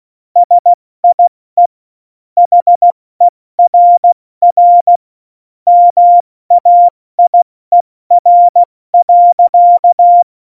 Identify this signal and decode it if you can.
Morse code, heard as a keyed tone, about 16 wpm, 700 Hz. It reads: SIE HERR MAIER.
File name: Morsekurs / Witze / Witz 22. September 2020, 70 F80